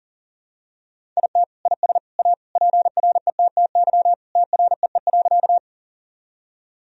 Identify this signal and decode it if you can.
Morse code, heard as a keyed tone, about 40 wpm, 700 Hz. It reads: IT IS A PRETTY TREE.
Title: Sentences from Top 1000 Words - Review Part 1 - 40wpm